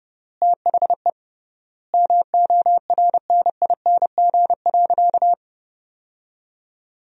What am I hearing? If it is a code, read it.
Morse code, heard as a keyed tone, about 30 wpm, 700 Hz. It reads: THE MORNING.
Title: 3x Word-Spacing - Sentences from Top 400 Words - 30wpm